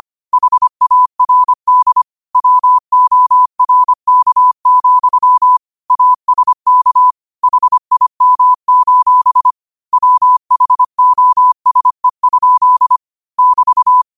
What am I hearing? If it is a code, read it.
Morse code, heard as a keyed tone, about 25 wpm, 1000 Hz. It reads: HARD WORK, ASK HIM: WHOSE? <BT>
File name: qotdfiles / 25wpm